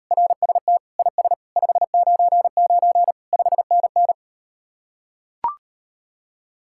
Morse code, heard as a keyed tone, about 38 wpm, 700 Hz. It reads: RST IS 599 5NN